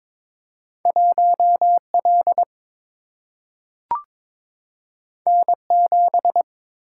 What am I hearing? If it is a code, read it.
Morse code, heard as a keyed tone, about 22 wpm, 700 Hz. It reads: 1L N7
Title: Single-Letter Single-Number 22wpm